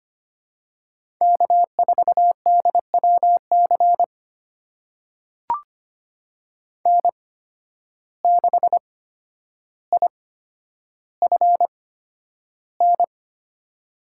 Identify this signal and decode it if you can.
Morse code, heard as a keyed tone, about 25 wpm, spaced as for 11 wpm, 700 Hz. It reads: K4DWC N6IFN
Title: USA Call Signs - 5 Characters - 8x Character Spacing - 25wpm